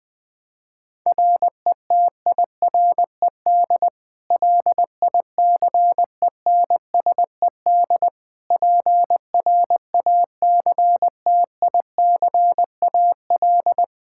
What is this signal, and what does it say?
Morse code, heard as a keyed tone, about 20 wpm, 700 Hz. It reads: RETIRED LICENSED PRACTICAL
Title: K1USN SST QSO Practice 20wpm